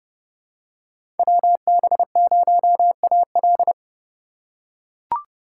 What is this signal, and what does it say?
Morse code, heard as a keyed tone, about 30 wpm, 700 Hz. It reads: WB0AL
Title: USA Call Signs - 5 Characters - 3x Character Spacing - 30wpm